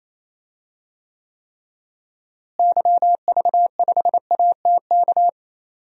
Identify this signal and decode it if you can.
Morse code, heard as a keyed tone, about 28 wpm, 700 Hz. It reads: YV5ATK
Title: DXCC-100 Call Signs - 6 Characters - Repeated 3x - 28wpm